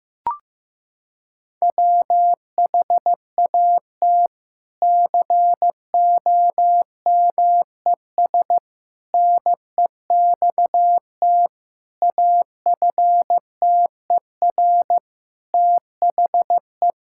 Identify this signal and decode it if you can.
Morse code, heard as a keyed tone, about 15 wpm, 700 Hz. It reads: WHAT COMES NEXT AFTER THE
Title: Sentences from Top 1000 Words - Review Part 5 - 15wpm